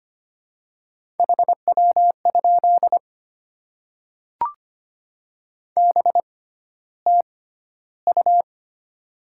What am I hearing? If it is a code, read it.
Morse code, heard as a keyed tone, about 25 wpm, spaced as for 13 wpm, 700 Hz. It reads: HW? BTU
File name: Top 20 QSO Elements - 6x Character Spacing - 25wpm